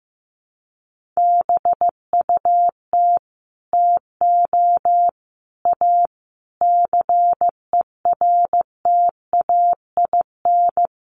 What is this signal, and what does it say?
Morse code, heard as a keyed tone, about 14 wpm, 700 Hz. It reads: BUT TO A CERTAIN
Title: Sentences from Top 1500 Words 15wpm